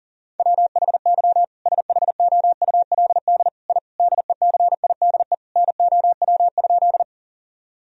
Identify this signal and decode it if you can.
Morse code, heard as a keyed tone, about 40 wpm, 700 Hz. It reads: WHY SHOULD I DECIDE NOW?
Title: Sentences from Top 1000 Words - Review Part 3 - 40wpm